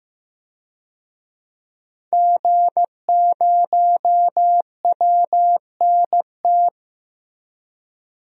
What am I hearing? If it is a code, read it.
Morse code, heard as a keyed tone, about 15 wpm, 700 Hz. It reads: G0WNT
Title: DXCC-100 Call Signs - 5 Characters - Repeated 2x - 15wpm